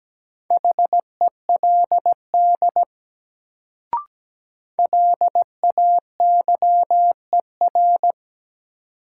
Morse code, heard as a keyed tone, about 17 wpm, 700 Hz. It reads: HELD LAYER
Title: Sentences from Top 1600 Words - New Word Drill Alt - 17wpm